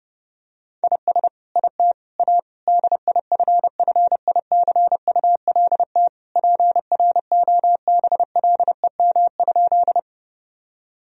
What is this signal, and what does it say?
Morse code, heard as a keyed tone, about 30 wpm, 700 Hz. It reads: IS IT A DIFFICULT PROBLEM?
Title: Sentences from Top 1000 Words - Review Part 2 - 30wpm